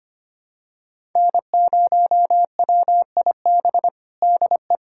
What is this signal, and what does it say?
Morse code, heard as a keyed tone, about 25 wpm, 700 Hz. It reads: N0WIB DE
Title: USA Callsign DE Callsign - 5 Characters - 25wpm